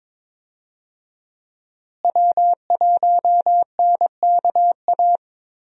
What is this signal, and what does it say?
Morse code, heard as a keyed tone, about 22 wpm, 700 Hz. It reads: W1NKA
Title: USA Call Signs - 5 Characters - Repeated 3x - 22wpm